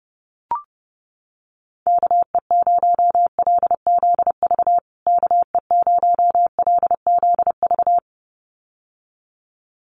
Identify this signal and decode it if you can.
Morse code, heard as a keyed tone, about 30 wpm, 700 Hz. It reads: KE0LZV KE0LZV
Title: USA Call Signs Repeated 2x 30wpm